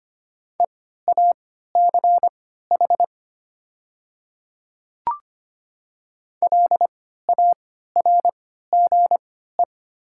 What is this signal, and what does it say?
Morse code, heard as a keyed tone, about 25 wpm, spaced as for 14 wpm, 700 Hz. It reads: EACH LARGE
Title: 3x Character Spacing - Top 200 Words - 25wpm